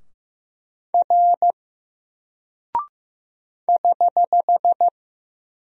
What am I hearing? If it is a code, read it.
Morse code, heard as a keyed tone, about 15 wpm, 700 Hz. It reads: R <HH>
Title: Single Character (Letter:Number:Punctuation:Prosign) 15wpm